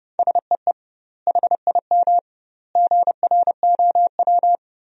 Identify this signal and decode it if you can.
Morse code, heard as a keyed tone, about 30 wpm, 700 Hz. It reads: SEE HIM GROW
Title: 2x Word-Spacing - Sentences from Top 300 Words - 30wpm